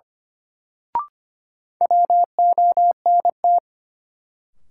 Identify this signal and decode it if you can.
Morse code, heard as a keyed tone, about 25 wpm, 700 Hz. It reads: WONT
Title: Sentences from Top 1000 Words - Review Part 6 - New Word Drill - 25wpm